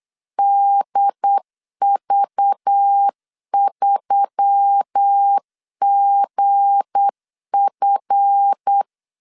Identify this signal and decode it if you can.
Morse code, heard as a keyed tone, about 8 wpm, 800 Hz. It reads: DV3GF
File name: LEZIONI / 7 GF9Z / 7 GF9 8wpm 3545